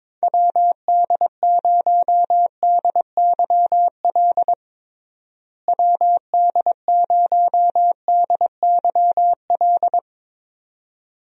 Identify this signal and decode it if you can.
Morse code, heard as a keyed tone, about 22 wpm, 700 Hz. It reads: WD0DYL WD0DYL
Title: USA Call Signs - 6 Characters - Repeated 2x - 22wpm